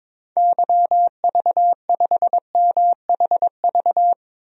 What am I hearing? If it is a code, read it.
Morse code, heard as a keyed tone, about 22 wpm, 700 Hz. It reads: YV5MHV
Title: DXCC-100 Call Signs - 6 Characters - Repeated 3x - 22wpm